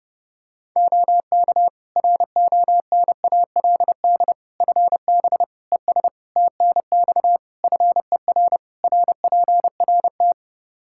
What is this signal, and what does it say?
Morse code, heard as a keyed tone, about 30 wpm, 700 Hz. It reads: OK RONALD FB ES TNX FER RPRT
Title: Practice QSO - Exchange 2 - Info - 30wpm